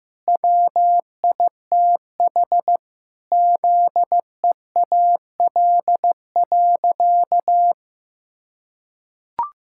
Morse code, heard as a keyed tone, about 15 wpm, 700 Hz. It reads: WITH ZEAL.
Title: Pangram Challenge v2 - 15wpm